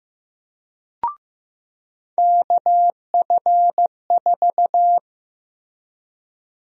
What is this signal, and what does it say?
Morse code, heard as a keyed tone, about 15 wpm, 700 Hz. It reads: KF4
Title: USA Call Sign Prefixes - Repeated 2x - 15wpm